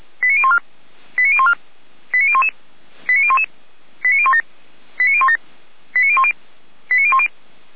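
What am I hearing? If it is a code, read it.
Five-tone selective call (ZVEI), heard as quick runs of five tones, tones of about 70 ms. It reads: 89014 89014 89011 89011 89017 89017 89010 89010
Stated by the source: ZVEI five-tone call 89060 received by SDR.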